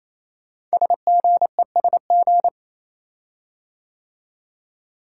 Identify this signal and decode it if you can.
Morse code, heard as a keyed tone, about 28 wpm, 700 Hz. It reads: SGESG
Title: Code Groups - 5 Letters - 28wpm